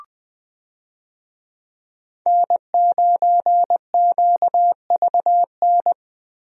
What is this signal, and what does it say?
Morse code, heard as a keyed tone, about 20 wpm, 700 Hz. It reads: N9QVN
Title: USA Call Signs - 5 Characters - Repeated 2x - 20wpm